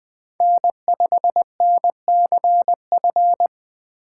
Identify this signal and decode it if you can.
Morse code, heard as a keyed tone, about 20 wpm, 700 Hz. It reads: N5NCF
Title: USA Call Signs - 5 Characters - 2x Character Spacing - 20wpm